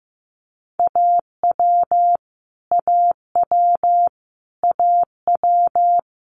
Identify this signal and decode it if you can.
Morse code, heard as a keyed tone, about 15 wpm, 700 Hz. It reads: AW AW AW
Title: Multi-letter Phonemes Repeated 3x 15wpm